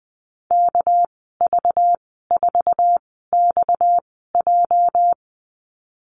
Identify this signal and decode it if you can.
Morse code, heard as a keyed tone, about 20 wpm, spaced as for 15 wpm, 700 Hz. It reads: KV4XJ
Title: 231 Lesson 37g - Letter x - Call Signs 20wpm